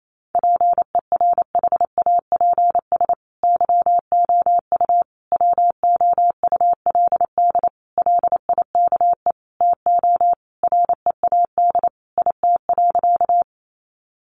Morse code, heard as a keyed tone, about 28 wpm, 700 Hz. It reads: PERHAPS YOU WOULD LIKE TO READ IT.
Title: Sentences from Top 500 Words 28wpm